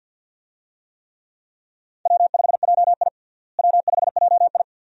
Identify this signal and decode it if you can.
Morse code, heard as a keyed tone, about 51 wpm, 700 Hz. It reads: W5JI W5JI
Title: USA Call Signs - 4 Characters - Repeated 3x - 50wpm